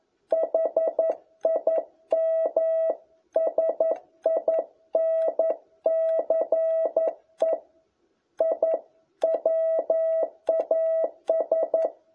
Morse code, heard as a keyed tone, about 11 wpm, 660 Hz. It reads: HIM SINCE I WAS
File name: jh8cbh / 2021cw